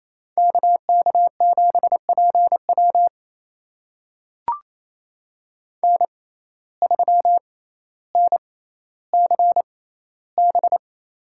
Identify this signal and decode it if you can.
Morse code, heard as a keyed tone, about 28 wpm, spaced as for 14 wpm, 700 Hz. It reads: KK7PW N3NCB
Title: USA Call Signs - 5 Characters - 6x Character Spacing - 28wpm